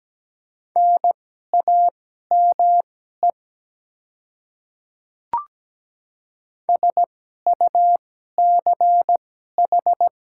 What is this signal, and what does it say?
Morse code, heard as a keyed tone, about 17 wpm, spaced as for 12 wpm, 700 Hz. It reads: NAME SUCH
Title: 2x Character Spacing - Top 300 Words - 17wpm